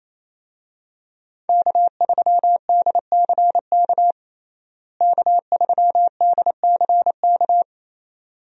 Morse code, heard as a keyed tone, about 27 wpm, 700 Hz. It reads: K3DCK K3DCK
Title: USA Call Signs - 5 Characters - Repeated 2x - 28wpm